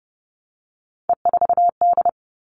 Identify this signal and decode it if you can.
Morse code, heard as a keyed tone, about 30 wpm, 700 Hz. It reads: E4D
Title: All Cut Numbers - 3-Digit 30wpm